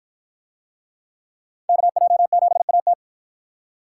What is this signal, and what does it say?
Morse code, heard as a keyed tone, about 53 wpm, 700 Hz. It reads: KJ7AT